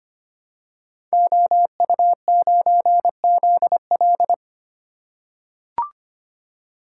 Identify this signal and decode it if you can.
Morse code, heard as a keyed tone, about 25 wpm, 700 Hz. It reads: OU9ZL
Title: Code Groups - 5 Letter:Numbers - 25wpm